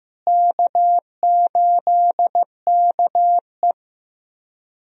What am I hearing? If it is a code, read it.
Morse code, heard as a keyed tone, about 15 wpm, 700 Hz. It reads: K8KE
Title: USA Call Signs - 4 Characters - Repeated 3x - 15wpm